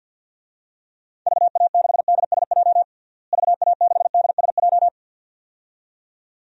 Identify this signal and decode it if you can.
Morse code, heard as a keyed tone, about 51 wpm, 700 Hz. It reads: VA6DSJ VA6DSJ
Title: DXCC-100 Call Signs - 6 Characters - Repeated 2x - 50wpm